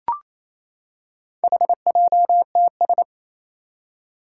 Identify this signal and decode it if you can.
Morse code, heard as a keyed tone, about 29 wpm, 700 Hz. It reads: HJTS
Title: Four Challenging Letter-Number Combinations 28wpm